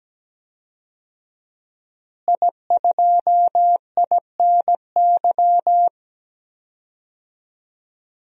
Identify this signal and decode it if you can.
Morse code, heard as a keyed tone, about 17 wpm, 700 Hz. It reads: I2INY
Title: DXCC-100 Call Signs - 5 Characters - Repeated 3x - 17wpm